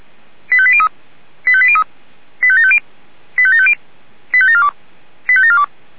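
Five-tone selective call (ZVEI), heard as quick runs of five tones, tones of about 70 ms. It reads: 85803 85803 85750 85750 85742 85742
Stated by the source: Five-tone ZVEI selective call received by SDR.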